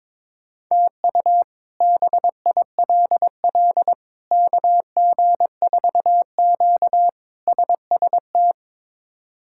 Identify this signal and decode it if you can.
Morse code, heard as a keyed tone, about 23 wpm, 700 Hz. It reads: TU BILL KG4Q SST
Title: K1USN SST QSO Practice - 22wpm